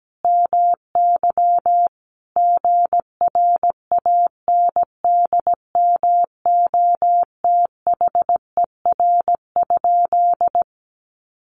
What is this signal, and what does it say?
Morse code, heard as a keyed tone, about 17 wpm, 700 Hz. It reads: MY GRANDMOTHER?
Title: Sentences from Top 1200 Words 17wpm